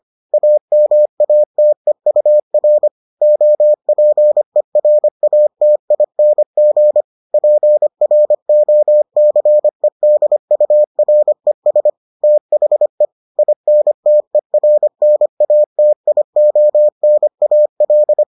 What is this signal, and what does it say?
Morse code, heard as a keyed tone, about 25 wpm, 600 Hz. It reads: AMATEUR OPERATING PROCEDURES THE INTERNATIONAL